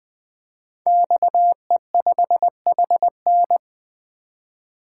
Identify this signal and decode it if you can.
Morse code, heard as a keyed tone, about 20 wpm, 700 Hz. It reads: XE5HN
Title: Code Groups - 5 Letter:Numbers - 20wpm